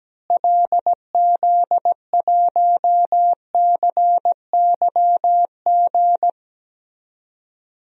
Morse code, heard as a keyed tone, about 17 wpm, 700 Hz. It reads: LZ1CYG
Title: DXCC-100 Call Signs - 6 Characters - Repeated 2x - 17wpm